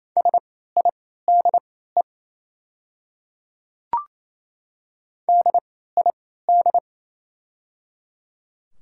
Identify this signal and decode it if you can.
Morse code, heard as a keyed tone, about 28 wpm, spaced as for 16 wpm, 700 Hz. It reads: SIDE DID
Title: 3x Character Spacing - Top 100 Words - 28wpm